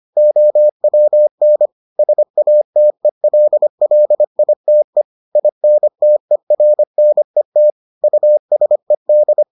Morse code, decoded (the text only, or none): OWN SATELLITE INTERNET USED